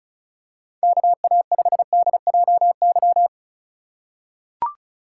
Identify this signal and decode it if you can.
Morse code, heard as a keyed tone, about 36 wpm, 700 Hz. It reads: KA5DJY
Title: USA Call Signs - 6 Characters - 4x Character Spacing - 35wpm